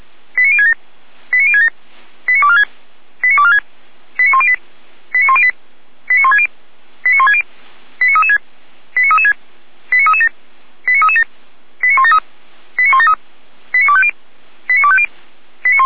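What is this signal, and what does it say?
Five-tone selective call (ZVEI), heard as quick runs of five tones, tones of about 70 ms. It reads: 89067 89067 89246 89246 89108 89108 89160 89160 89336 89336 89337 89337 89173 89173 89250 89250
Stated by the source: ZVEI five-tone selective call received with an SDR.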